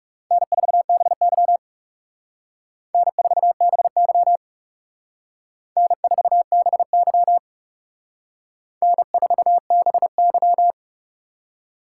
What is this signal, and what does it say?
Morse code, heard as a keyed tone, about 36 wpm, 700 Hz. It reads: N4BY N4BY N4BY N4BY
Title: Speed Racing - USA 1x2 Call Signs 30wpm